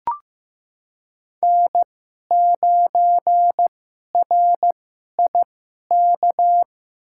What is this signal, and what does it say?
Morse code, heard as a keyed tone, about 15 wpm, spaced as for 11 wpm, 700 Hz. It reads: N9RIK
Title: USA Call Signs - 5 Characters - 2x Character Spacing - 15wpm